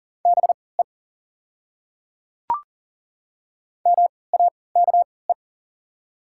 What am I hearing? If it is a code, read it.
Morse code, heard as a keyed tone, about 40 wpm, spaced as for 23 wpm, 690 Hz. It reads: BE MAKE
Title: 3x Character Spacing - Top 300 Words - 40wpm